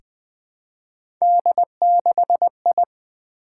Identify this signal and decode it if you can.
Morse code, heard as a keyed tone, about 20 wpm, 700 Hz. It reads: D6I
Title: Three Letter-Number Combinations - 20wpm